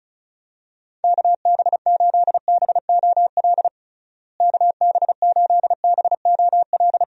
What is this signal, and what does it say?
Morse code, heard as a keyed tone, about 36 wpm, 700 Hz. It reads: KB8BOL KB8BOL
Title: USA Call Signs - 6 Characters - Repeated 2x - 35wpm